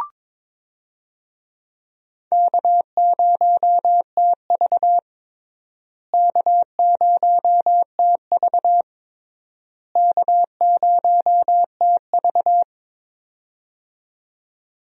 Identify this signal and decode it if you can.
Morse code, heard as a keyed tone, about 22 wpm, 700 Hz. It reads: K0TV K0TV K0TV
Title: USA Call Signs - 4 Characters - Repeated 3x - 22wpm